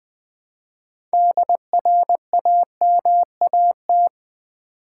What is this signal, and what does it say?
Morse code, heard as a keyed tone, about 20 wpm, 700 Hz. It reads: DRAMAT
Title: Build A Word - 6 Letters At A Time - 12-Letter Words - 20wpm